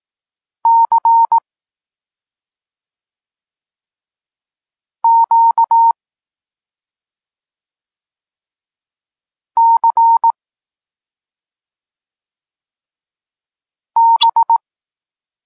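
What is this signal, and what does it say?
Morse code, heard as a keyed tone, about 18 wpm, 920 Hz. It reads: C Q C B
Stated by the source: Polish Morse lesson teaching the letter C, from SP9MOA Niepołomicki Klub Krótkofalowców.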